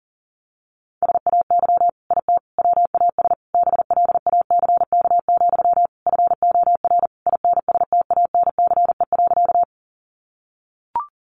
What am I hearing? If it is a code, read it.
Morse code, heard as a keyed tone, about 40 wpm, 700 Hz. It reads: SAY IT WAS BLACK, FOR INSTANCE.
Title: Sentences from Top 800 Words 40wpm